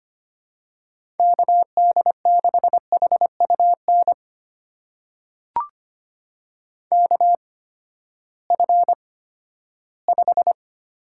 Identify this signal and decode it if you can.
Morse code, heard as a keyed tone, about 25 wpm, spaced as for 14 wpm, 700 Hz. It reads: KD6HUN KF5
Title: USA Call Signs - 6 Characters - 8x Character Spacing - 25wpm